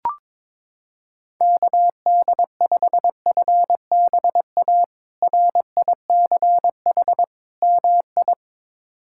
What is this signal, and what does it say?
Morse code, heard as a keyed tone, about 22 wpm, 700 Hz. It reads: KD5FBA RICH MI
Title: K1USN SST QSO Practice - 22wpm